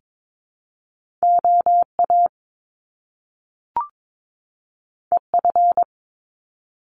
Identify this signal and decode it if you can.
Morse code, heard as a keyed tone, about 22 wpm, 700 Hz. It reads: OA EF
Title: Top Two Letter Combinations in Words 22wpm